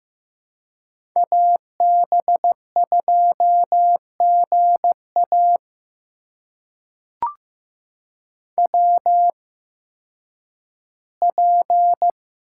Morse code, heard as a keyed tone, about 15 wpm, spaced as for 10 wpm, 700 Hz. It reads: AB2GA WP
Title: USA Call Signs - 5 Characters - 8x Character Spacing - 15wpm